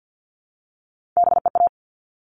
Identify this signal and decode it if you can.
Morse code, heard as a keyed tone, about 50 wpm, 700 Hz. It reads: BEA
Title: All Cut Numbers - 3-Digit 50wpm